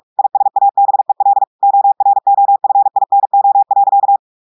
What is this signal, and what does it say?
Morse code, heard as a keyed tone, about 46 wpm, 800 Hz. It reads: ISABEL OROFINO.
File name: portugais texte en clair 1 vitesse 45 mots minutes